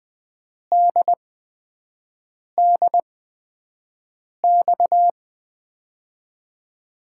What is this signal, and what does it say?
Morse code, heard as a keyed tone, about 20 wpm, spaced as for 5 wpm, 700 Hz. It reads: DDX
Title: USA Call Signs - 6 Characters - 8x Character Spacing - 20wpm